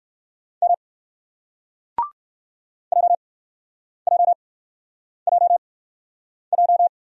Morse code, heard as a keyed tone, about 53 wpm, spaced as for 9 wpm, 700 Hz. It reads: M JJJJ
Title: Speed-Racing Single Letter-Number 45wpm